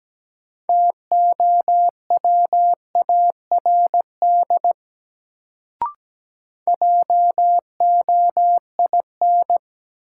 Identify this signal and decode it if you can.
Morse code, heard as a keyed tone, about 17 wpm, 700 Hz. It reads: TOWARD JOIN
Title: Sentences from Top 1000 Words - Review Part 6 - New Word Drill - 17wpm